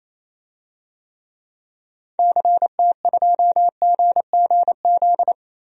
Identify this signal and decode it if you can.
Morse code, heard as a keyed tone, about 28 wpm, 700 Hz. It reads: CT2GGZ
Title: DXCC-100 Call Signs - 6 Characters - 28wpm